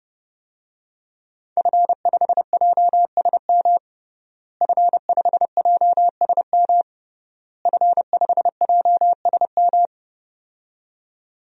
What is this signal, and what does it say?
Morse code, heard as a keyed tone, about 32 wpm, 700 Hz. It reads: F5JSM F5JSM F5JSM
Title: DXCC-100 Call Signs - 5 Characters - Repeated 3x - 30wpm